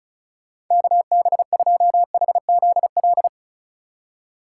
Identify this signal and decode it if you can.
Morse code, heard as a keyed tone, about 45 wpm, 700 Hz. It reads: KB2HZL